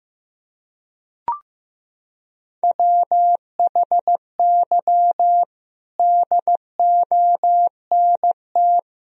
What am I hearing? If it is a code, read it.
Morse code, heard as a keyed tone, about 15 wpm, 700 Hz. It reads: WHY DONT
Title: Sentences from Top 300 Words and Abbreviations 15wpm